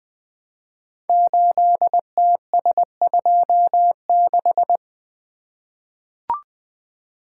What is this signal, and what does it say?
Morse code, heard as a keyed tone, about 20 wpm, 700 Hz. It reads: 8TS26
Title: Code Groups - 5 Letter:Numbers - 20wpm